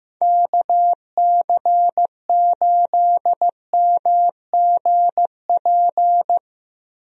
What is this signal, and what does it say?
Morse code, heard as a keyed tone, about 16 wpm, 700 Hz. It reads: KC8MGP
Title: USA Call Signs - 6 Characters - Repeated 2x - 15wpm